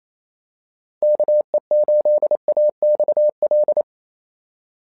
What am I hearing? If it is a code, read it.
Morse code, heard as a keyed tone, about 28 wpm, 600 Hz. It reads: KE8AXL
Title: SOTA QSO Practice 28wpm